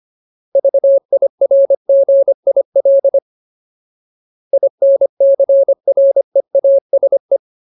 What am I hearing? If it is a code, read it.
Morse code, heard as a keyed tone, about 25 wpm, 550 Hz. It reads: VIRGIL INCREASE
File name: Ultimate Word List / 25WPM Ultimate